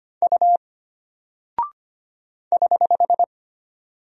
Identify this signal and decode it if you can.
Morse code, heard as a keyed tone, about 25 wpm, 700 Hz. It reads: U <HH>
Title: Single Character (Letter:Number:Punctuation:Prosign) Alt 25wpm